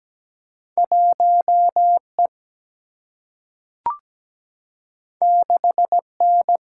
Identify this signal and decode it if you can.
Morse code, heard as a keyed tone, about 17 wpm, 700 Hz. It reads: 1E 6N
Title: Single-Letter Single-Number 17wpm